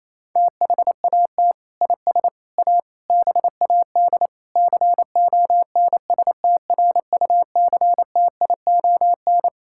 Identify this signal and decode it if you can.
Morse code, heard as a keyed tone, about 28 wpm, 700 Hz. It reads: THAT IS A BAD CONSTRUCTION
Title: Sentences from Top 1600 Words 28wpm